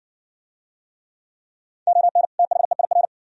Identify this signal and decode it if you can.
Morse code, heard as a keyed tone, about 64 wpm, 700 Hz. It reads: ON THEIR